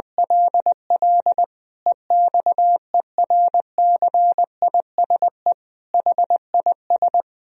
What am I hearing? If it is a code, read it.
Morse code, heard as a keyed tone, about 21 wpm, 700 Hz. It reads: LL EXERCISE HIS
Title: Sentences from Top 1000 Words - Review Part 4 - 20wpm